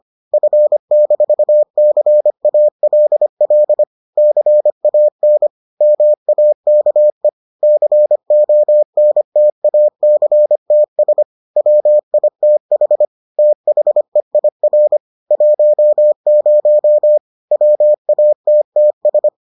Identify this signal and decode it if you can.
Morse code, heard as a keyed tone, about 25 wpm, 600 Hz. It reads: F-CALL CAN MAKE CONTACTS WITH THEIR 10 WATTS